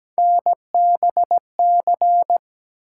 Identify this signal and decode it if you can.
Morse code, heard as a keyed tone, about 17 wpm, 700 Hz. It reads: NBC
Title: Sets of 5 Words Encore - New Word Drill - 17wpm